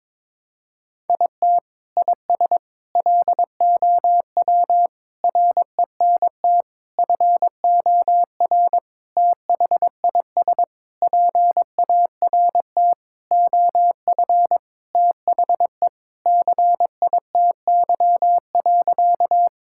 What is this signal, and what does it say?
Morse code, heard as a keyed tone, about 22 wpm, 700 Hz. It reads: IT IS LOW RENT FOR THIS PART OF THE CITY.
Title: Sentences from Top 1000 Words - Review Part 5 - 22wpm